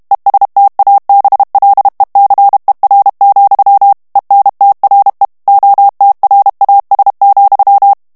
Morse code, heard as a keyed tone, about 32 wpm, 800 Hz. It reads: ESTABLECER, ENTRE OTRAS,
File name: ESPAGNOL TEXTE EN CLAIR 2 VITESSE 30 MOTS MINUTES